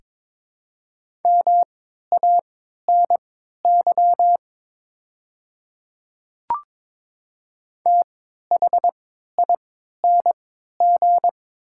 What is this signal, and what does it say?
Morse code, heard as a keyed tone, about 22 wpm, spaced as for 12 wpm, 700 Hz. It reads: MANY THING
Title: 3x Character Spacing - Top 200 Words - 22wpm